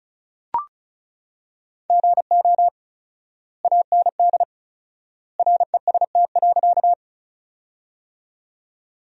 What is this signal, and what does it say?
Morse code, heard as a keyed tone, about 36 wpm, 700 Hz. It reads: GO AND REST.
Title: 4x Word-Spacing - Sentences from Top 500 Words - 35wpm